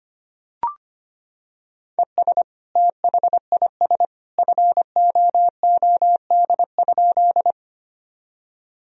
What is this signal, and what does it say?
Morse code, heard as a keyed tone, about 25 wpm, 700 Hz. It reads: ES THIS FOOD?
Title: Sentences from Top 300 Words and Abbreviations 25wpm